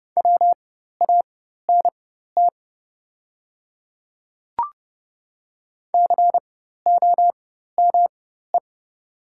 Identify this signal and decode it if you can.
Morse code, heard as a keyed tone, about 30 wpm, spaced as for 14 wpm, 700 Hz. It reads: WANT COME
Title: 4x Character Spacing - Top 200 Words - 30wpm